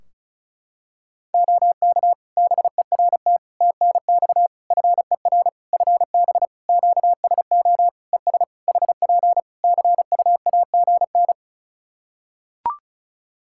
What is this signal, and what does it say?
Morse code, heard as a keyed tone, about 36 wpm, 700 Hz. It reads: OK BERT TNX FER FB QSO ES HP CUAGN
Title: Practice QSO - Exchange 4 - Ending - 35wpm